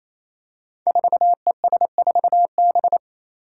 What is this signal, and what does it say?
Morse code, heard as a keyed tone, about 28 wpm, 700 Hz. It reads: 4ES4B
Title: Five Challenging Letter-Number Combinations 28wpm